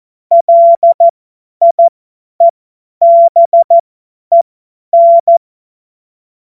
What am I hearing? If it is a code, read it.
Morse code, heard as a keyed tone, about 14 wpm, spaced as for 10 wpm, 680 Hz. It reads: LIEBEN
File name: Morsekurs / Witze / Witz 24. Januar 2023 50 F70